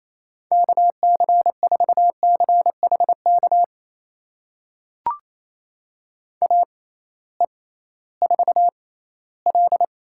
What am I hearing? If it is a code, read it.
Morse code, heard as a keyed tone, about 28 wpm, spaced as for 16 wpm, 700 Hz. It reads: KC4CHK AE4L
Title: USA Call Signs - 6x Character Spacing - 28wpm